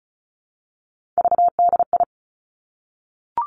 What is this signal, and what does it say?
Morse code, heard as a keyed tone, about 36 wpm, 700 Hz. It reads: VDI
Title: Three Challenging Letter-Number Combinations 35wpm